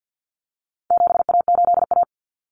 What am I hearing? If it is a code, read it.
Morse code, heard as a keyed tone, about 50 wpm, 700 Hz. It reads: 7A8A
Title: Common Cut Numbers - 4-Digit 50wpm